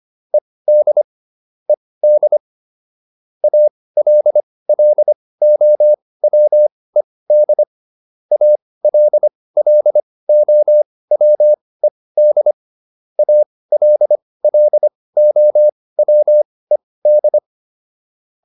Morse code, heard as a keyed tone, about 25 wpm, spaced as for 18 wpm, 600 Hz. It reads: ED ED ALLOWED ALLOWED ALLOWED